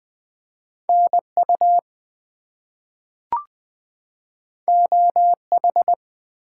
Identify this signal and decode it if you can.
Morse code, heard as a keyed tone, about 20 wpm, 700 Hz. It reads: NU OH
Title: US States and Canadian Provinces 20wpm